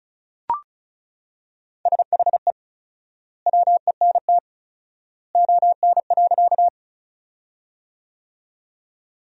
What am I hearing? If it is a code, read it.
Morse code, heard as a keyed tone, about 36 wpm, 700 Hz. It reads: SHE WENT ON.
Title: 4x Word-Spacing - Sentences from Top 200 Words - 35wpm